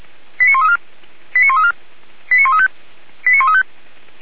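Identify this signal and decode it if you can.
Five-tone selective call (ZVEI), heard as quick runs of five tones, tones of about 70 ms. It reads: 89134 89134 89135 89135
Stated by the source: Five-tone ZVEI selective call received by SDR, from the feuerwehr-orlamuende.org fire brigade audio server.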